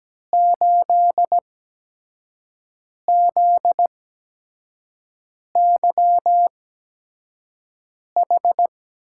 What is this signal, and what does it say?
Morse code, heard as a keyed tone, about 17 wpm, spaced as for 5 wpm, 700 Hz. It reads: 8ZYH
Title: USA Call Signs - 8x Character Spacing - 17wpm